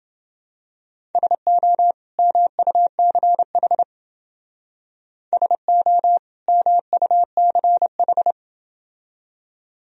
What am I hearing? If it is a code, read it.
Morse code, heard as a keyed tone, about 29 wpm, 700 Hz. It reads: SO MUCH SO MUCH